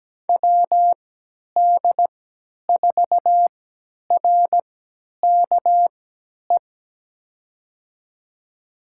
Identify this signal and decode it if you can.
Morse code, heard as a keyed tone, about 17 wpm, spaced as for 10 wpm, 700 Hz. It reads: WD4RKE
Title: USA Call Signs - 3x Character Spacing - 17wpm